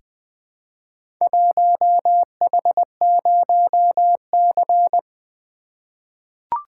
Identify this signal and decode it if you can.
Morse code, heard as a keyed tone, about 20 wpm, 700 Hz. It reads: 1H0C
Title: Four Letter-Number Combinations - 20wpm